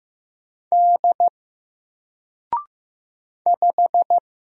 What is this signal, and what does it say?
Morse code, heard as a keyed tone, about 15 wpm, 700 Hz. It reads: D 5
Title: Single Letter-Number 15wpm